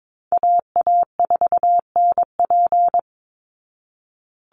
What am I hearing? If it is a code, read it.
Morse code, heard as a keyed tone, about 22 wpm, 700 Hz. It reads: AA4NP
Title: Call Signs Repeated 2x - 22wpm